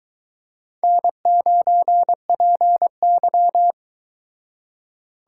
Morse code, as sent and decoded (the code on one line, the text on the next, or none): -. ----. .--. -.--
N9PY